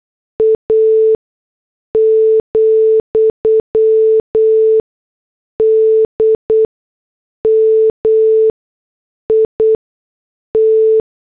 Morse code, heard as a keyed tone, about 8 wpm, spaced as for 6 wpm, 440 Hz. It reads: A, DMIT